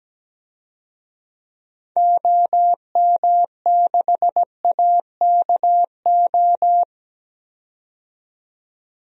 Morse code, heard as a keyed tone, about 17 wpm, 700 Hz. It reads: OM6AKO
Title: DXCC-100 Call Signs - 6 Characters - 17wpm